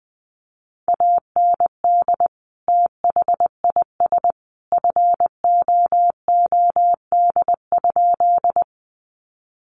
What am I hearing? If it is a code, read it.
Morse code, heard as a keyed tone, about 20 wpm, 700 Hz. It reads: AND THIS FOOD?
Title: Sentences from Top 300 Words 20wpm